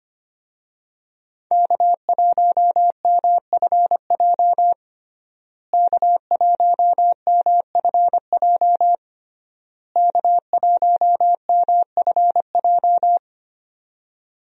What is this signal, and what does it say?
Morse code, heard as a keyed tone, about 25 wpm, 700 Hz. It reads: K1MFJ K1MFJ K1MFJ
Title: USA Call Signs - 5 Characters - Repeated 3x - 25wpm